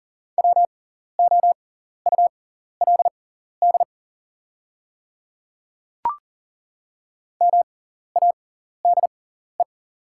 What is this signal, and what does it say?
Morse code, heard as a keyed tone, about 42 wpm, spaced as for 14 wpm, 690 Hz. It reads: WOULD MADE
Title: 6x Character Spacing - Top 200 Words - 40wpm